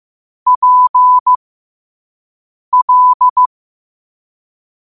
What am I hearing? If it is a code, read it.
Morse code, heard as a keyed tone, about 15 wpm, spaced as for 5 wpm, 1000 Hz. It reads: PL
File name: qotdfiles / 5wpm